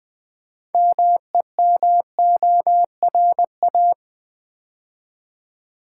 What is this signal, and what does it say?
Morse code, heard as a keyed tone, about 20 wpm, 700 Hz. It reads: MEMORA
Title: Build A Word - 3 Letters At A Time - 9-Letter Words - 20wpm